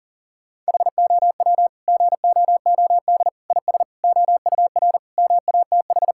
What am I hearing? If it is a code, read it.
Morse code, heard as a keyed tone, about 37 wpm, 700 Hz. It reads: HOW GOOD IS OUR MATH
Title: Sentences from Top 1000 Words - Review Part 6 - 40wpm